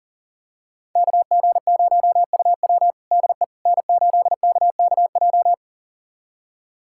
Morse code, heard as a keyed tone, about 37 wpm, 700 Hz. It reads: KG0UW DE N8KKJ
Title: USA Callsign DE Callsign - 5 Characters - 40wpm